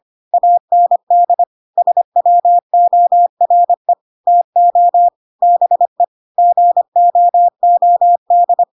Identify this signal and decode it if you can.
Morse code, heard as a keyed tone, about 25 wpm, 700 Hz. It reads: AND SWORE TO BE GOOD
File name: Pinocchio / Pinocchio ch2 25 25 ik1hge